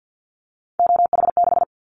Morse code, heard as a keyed tone, about 50 wpm, 700 Hz. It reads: OH6
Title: Callsign Prefixes for Top 50 DXCC - 50wpm